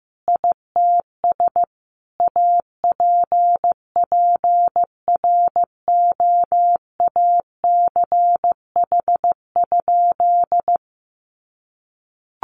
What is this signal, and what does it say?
Morse code, heard as a keyed tone, about 15 wpm, 700 Hz. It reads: ITS APPROACH?
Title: Sentences from Top 1500 Words 15wpm